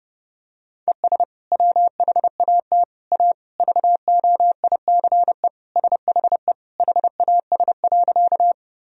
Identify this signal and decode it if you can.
Morse code, heard as a keyed tone, about 32 wpm, 700 Hz. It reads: ES WHAT A VOICE SHE HAS.
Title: Sentences from Top 500 Words and Abbreviations 30wpm